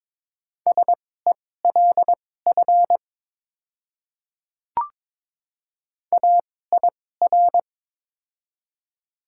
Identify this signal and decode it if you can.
Morse code, heard as a keyed tone, about 21 wpm, spaced as for 16 wpm, 700 Hz. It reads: SELF AIR
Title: 2x Character Spacing - Top 200 Words - 22wpm